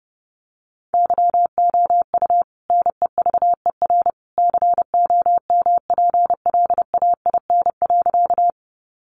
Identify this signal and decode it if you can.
Morse code, heard as a keyed tone, about 30 wpm, 700 Hz. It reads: YOU NEVER COMPLAIN.
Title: Sentences from Top 1300 Words 30wpm